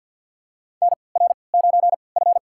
Morse code, heard as a keyed tone, about 47 wpm, spaced as for 28 wpm, 700 Hz. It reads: NR9F
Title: USA Call Signs - 4 Characters - 3x Character Spacing - 50wpm